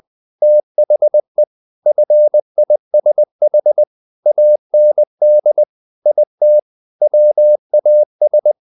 Morse code, heard as a keyed tone, about 20 wpm, 600 Hz. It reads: THE FISH AND IT WAS
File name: the old man and the sea / the old man and the sea part 1 20wpm